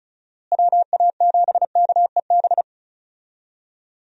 Morse code, heard as a keyed tone, about 36 wpm, 700 Hz. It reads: WA7KEB
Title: USA Call Signs - 6 Characters - Repeated 2x - 35wpm